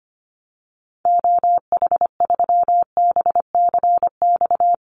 Morse code, heard as a keyed tone, about 25 wpm, 700 Hz. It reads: OH3BCX
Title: Call Signs - 25wpm